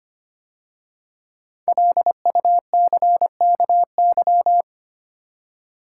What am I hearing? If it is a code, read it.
Morse code, heard as a keyed tone, about 25 wpm, 700 Hz. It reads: LUCKY
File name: 5 Letter Words - First 3 Letters Given - 25wpm